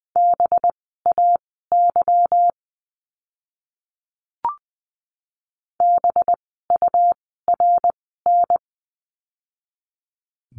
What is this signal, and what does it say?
Morse code, heard as a keyed tone, about 20 wpm, spaced as for 14 wpm, 700 Hz. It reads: BAY BURN
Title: 185 Lesson 30f - Letter b - Words 20wpm